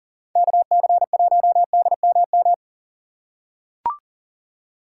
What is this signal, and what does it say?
Morse code, heard as a keyed tone, about 40 wpm, 700 Hz. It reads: KC1DMM
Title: USA Call Signs - 6 Characters - 6x Character Spacing - 40wpm